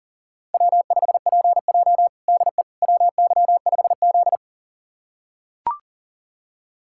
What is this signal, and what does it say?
Morse code, heard as a keyed tone, about 40 wpm, 700 Hz. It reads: W5PJ DE WY5Z
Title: USA Callsign DE Callsign - 4 Characters - 40wpm